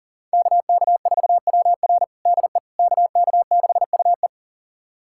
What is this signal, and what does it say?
Morse code, heard as a keyed tone, about 40 wpm, 700 Hz. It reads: KK4WR DE KK6UE
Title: USA Callsign DE Callsign - 5 Characters - 40wpm